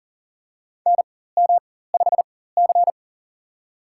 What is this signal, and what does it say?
Morse code, heard as a keyed tone, about 40 wpm, spaced as for 19 wpm, 700 Hz. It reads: NM5C
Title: USA Call Signs - 4 Characters - 4x Character Spacing - 40wpm